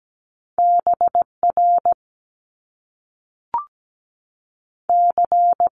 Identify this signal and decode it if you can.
Morse code, heard as a keyed tone, about 17 wpm, 700 Hz. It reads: B R C
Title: Top Two Letter Combinations in Words 17wpm